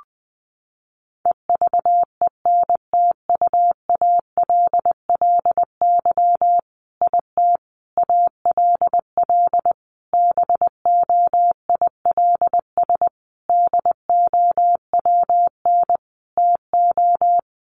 Morse code, heard as a keyed tone, about 20 wpm, 700 Hz. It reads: EVENTUALLY IT ALL BOILS DOWN TO
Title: A Sound Of Thunder 20wpm